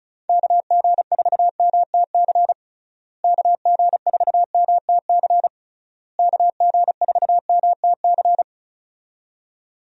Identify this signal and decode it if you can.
Morse code, heard as a keyed tone, about 34 wpm, 700 Hz. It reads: KG4MTC KG4MTC KG4MTC
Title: USA Call Signs - 6 Characters - Repeated 3x - 35wpm